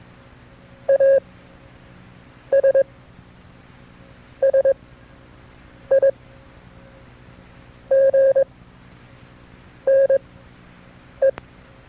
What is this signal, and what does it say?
Morse code, heard as a keyed tone, about 22 wpm, spaced as for 6 wpm, 540 Hz. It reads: ASSIGNE
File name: FreeFiles / CW-practice / KB6MT-tape2-side2